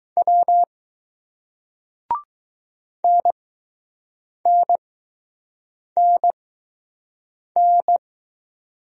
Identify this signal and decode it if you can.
Morse code, heard as a keyed tone, about 20 wpm, spaced as for 6 wpm, 700 Hz. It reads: W NNNN